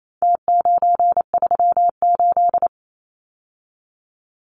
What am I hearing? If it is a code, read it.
Morse code, heard as a keyed tone, about 28 wpm, 700 Hz. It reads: T938
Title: Numbers - Contest Style - 4-Digit 28wpm